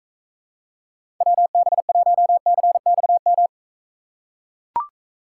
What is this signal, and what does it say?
Morse code, heard as a keyed tone, about 44 wpm, 700 Hz. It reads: WB1CXM